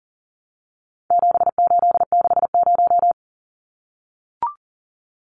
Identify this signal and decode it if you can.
Morse code, heard as a keyed tone, about 40 wpm, 700 Hz. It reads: 7860
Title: Numbers - 4-Digit 40wpm